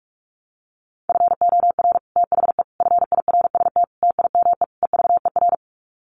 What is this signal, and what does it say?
Morse code, heard as a keyed tone, about 44 wpm, 700 Hz. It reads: FOR THE FIRST TIME EVER
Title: Sets of 5 Words Encore2 45wpm